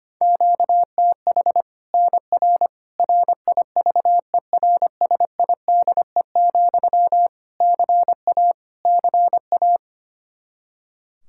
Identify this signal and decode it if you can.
Morse code, heard as a keyed tone, about 25 wpm, 700 Hz. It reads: QTH NR RIVERSIDE, CA CA
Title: Practice QSO - Exchange 1 - Beginning - 25wpm